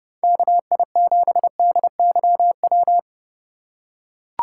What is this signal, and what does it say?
Morse code, heard as a keyed tone, about 32 wpm, 700 Hz. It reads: KI7DYW
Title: USA Call Signs - 6 Characters - 4x Character Spacing - 30wpm